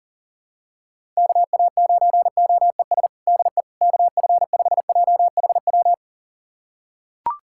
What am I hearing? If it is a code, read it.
Morse code, heard as a keyed tone, about 40 wpm, 700 Hz. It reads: KA9OES DE KF5JHW
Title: USA Callsign DE Callsign 40wpm